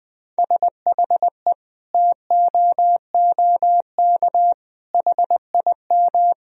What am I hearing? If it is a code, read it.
Morse code, heard as a keyed tone, about 20 wpm, 700 Hz. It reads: SHE TOOK HIM
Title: Sentences from Top 1000 Words - Review Part 4 - 20wpm